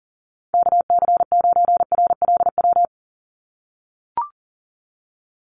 Morse code, heard as a keyed tone, about 40 wpm, 700 Hz. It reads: KC9RLW
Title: USA Call Signs 40wpm